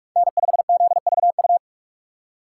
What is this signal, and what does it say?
Morse code, heard as a keyed tone, about 43 wpm, 700 Hz. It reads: N5ZVU